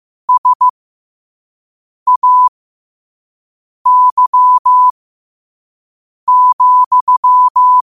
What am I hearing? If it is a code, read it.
Morse code, heard as a keyed tone, about 15 wpm, spaced as for 5 wpm, 1000 Hz. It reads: SAY,
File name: qotdfiles / 5wpm